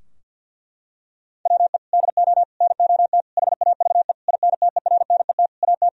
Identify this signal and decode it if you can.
Morse code, heard as a keyed tone, about 50 wpm, 700 Hz. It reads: WE DO NOT HAVE INTERNET AT
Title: Sentences from Top 1000 Words - Review Part 4 - 50wpm